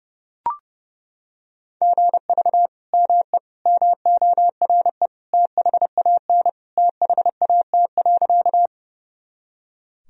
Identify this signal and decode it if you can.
Morse code, heard as a keyed tone, about 32 wpm, 700 Hz. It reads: GV ME MORE THAN THAT.
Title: Sentences from Top 200 Words and Abbreviations 30wpm